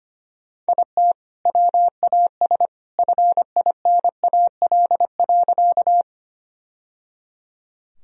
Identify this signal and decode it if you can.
Morse code, heard as a keyed tone, about 25 wpm, 700 Hz. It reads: IT WAS FINAL.
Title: Sentences from Top 1000 Words - Review Part 2 - 25wpm